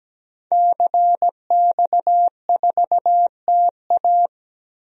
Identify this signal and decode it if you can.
Morse code, heard as a keyed tone, about 17 wpm, 700 Hz. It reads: CX4TA
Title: DXCC-100 Call Signs - 5 Characters - Repeated 3x - 17wpm